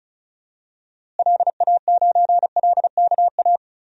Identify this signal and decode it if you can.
Morse code, heard as a keyed tone, about 36 wpm, 700 Hz. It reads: LA9LKA
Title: DXCC-100 Call Signs - 6 Characters - Repeated 2x - 35wpm